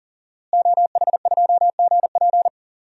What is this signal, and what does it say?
Morse code, heard as a keyed tone, about 40 wpm, 700 Hz. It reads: OH2GP
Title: DXCC-100 Call Signs - 5 Characters - Repeated 2x - 40wpm